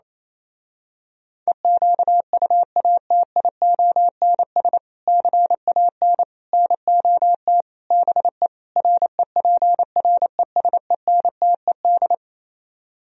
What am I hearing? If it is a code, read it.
Morse code, heard as a keyed tone, about 27 wpm, 700 Hz. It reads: EQUATIONS CAN NOT BE REPRESENTED
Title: Speed-Racing - Sets of 5 Words 28wpm